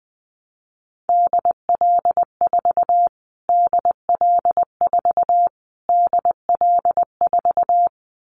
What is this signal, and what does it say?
Morse code, heard as a keyed tone, about 20 wpm, 700 Hz. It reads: DL4 DL4 DL4
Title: Callsign Prefixes for Top 25 DXCC - Repeated 3x - 20wpm